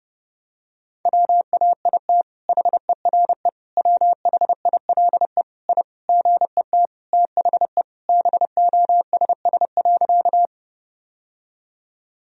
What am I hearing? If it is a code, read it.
Morse code, heard as a keyed tone, about 32 wpm, 700 Hz. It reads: WAIT HERE WHILE I GET THE BOSS.
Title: Sentences from Top 1000 Words - Review Part 4 - 30wpm